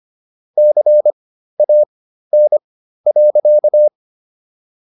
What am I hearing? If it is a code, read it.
Morse code, heard as a keyed tone, about 25 wpm, spaced as for 13 wpm, 600 Hz. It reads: CAN.